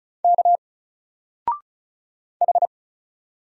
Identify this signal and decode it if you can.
Morse code, heard as a keyed tone, about 33 wpm, 700 Hz. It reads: K H
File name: Single-Letter-Number-35wpm